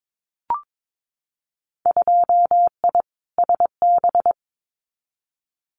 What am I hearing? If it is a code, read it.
Morse code, heard as a keyed tone, about 22 wpm, 700 Hz. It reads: 2I SB
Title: Winter Field Day Exchanges Encore 22wpm